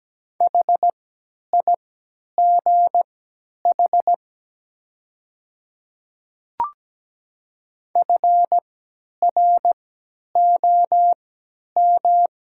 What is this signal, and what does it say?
Morse code, heard as a keyed tone, about 17 wpm, spaced as for 10 wpm, 700 Hz. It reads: HIGH FROM
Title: 3x Character Spacing - Top 500 Words - 17wpm